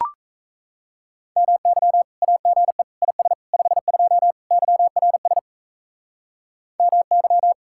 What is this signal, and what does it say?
Morse code, heard as a keyed tone, about 37 wpm, 700 Hz. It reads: MY AGE IS 52 YRS MY